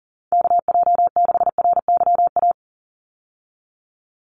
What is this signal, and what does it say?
Morse code, heard as a keyed tone, about 40 wpm, 700 Hz. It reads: KJ6RYA
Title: USA Call Signs 40wpm